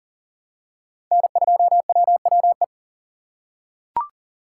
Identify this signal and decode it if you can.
Morse code, heard as a keyed tone, about 38 wpm, 700 Hz. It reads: N2WWE
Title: USA Call Signs - 5 Characters - 3x Character Spacing - 40wpm